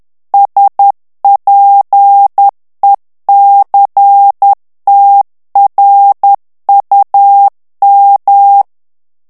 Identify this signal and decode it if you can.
Morse code, heard as a keyed tone, about 10 wpm, 800 Hz. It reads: SPECTRUM
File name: ANGLAIS texte clair 3 vitesse 10 mots minute